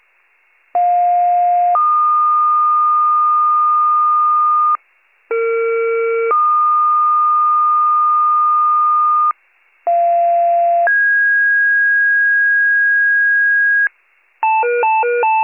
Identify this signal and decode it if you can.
Two-tone page, heard as a one-second tone, then a three-second tone, about 710 then 1220 Hz, about 460 then 1220 Hz, about 690 then 1650 Hz, with noise. It is heard from 0.7 to 13.9 s.